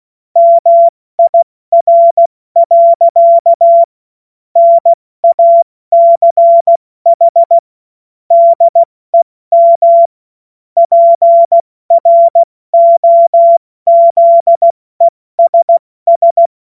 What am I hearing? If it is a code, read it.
Morse code, heard as a keyed tone, about 16 wpm, spaced as for 14 wpm, 680 Hz. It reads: MIR. NACH DEM PROZESS
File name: Morsekurs / Witze / Witz 29. August 2023 70 F80